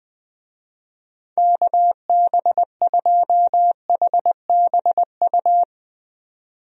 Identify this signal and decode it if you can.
Morse code, heard as a keyed tone, about 20 wpm, 700 Hz. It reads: KB2HBU
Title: USA Call Signs - 6 Characters - 20wpm